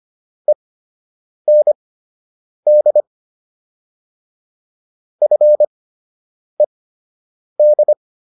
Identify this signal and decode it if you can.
Morse code, heard as a keyed tone, about 25 wpm, spaced as for 8 wpm, 600 Hz. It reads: END FED